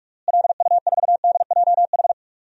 Morse code, heard as a keyed tone, about 46 wpm, 700 Hz. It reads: LU4DJH
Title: DXCC-100 Call Signs - 6 Characters - Repeated 2x - 45wpm